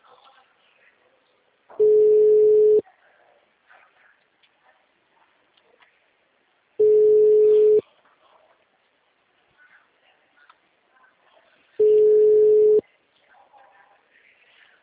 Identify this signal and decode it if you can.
Ringback tone (Europe, 425 Hz).